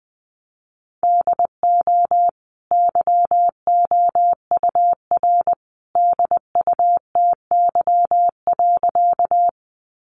Morse code, heard as a keyed tone, about 20 wpm, 700 Hz. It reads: DO YOUR DUTY.
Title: Sentences from Top 1500 Words 20wpm